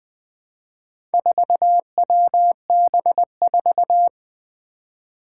Morse code, heard as a keyed tone, about 21 wpm, 700 Hz. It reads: 4WB4
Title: Four Challenging Letter-Number Combinations 20wpm